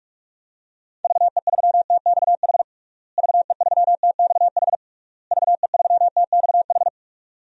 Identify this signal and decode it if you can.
Morse code, heard as a keyed tone, about 44 wpm, 700 Hz. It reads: VE3TXH VE3TXH VE3TXH
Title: DXCC-100 Call Signs - 6 Characters - Repeated 3x - 45wpm